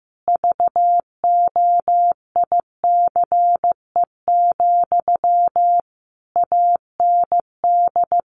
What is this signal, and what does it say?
Morse code, heard as a keyed tone, about 15 wpm, 700 Hz. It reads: VOICE, AND
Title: Sentences from Top 1100 Words 15wpm